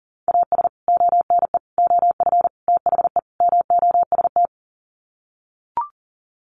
Morse code, heard as a keyed tone, about 40 wpm, 700 Hz. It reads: AS ONE OF THE MOST
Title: Sets of 5 Words 40wpm